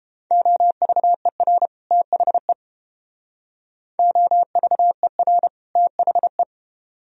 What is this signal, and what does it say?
Morse code, heard as a keyed tone, about 31 wpm, 700 Hz. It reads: OVER THE OVER THE